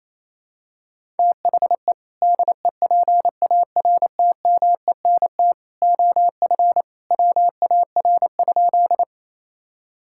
Morse code, heard as a keyed tone, about 28 wpm, 700 Hz. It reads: THE DEPARTMENT OF WAR?
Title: Sentences from Top 1000 Words - Review Part 5 - 28wpm